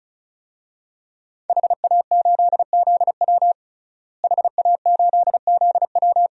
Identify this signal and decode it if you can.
Morse code, heard as a keyed tone, about 36 wpm, 700 Hz. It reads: HA8ZW HA8ZW
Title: DXCC-100 Call Signs - 5 Characters - Repeated 2x - 35wpm